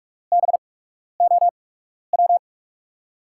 Morse code, heard as a keyed tone, about 43 wpm, spaced as for 12 wpm, 700 Hz. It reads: BOW
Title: USA Call Signs - 6 Characters - 8x Character Spacing - 45wpm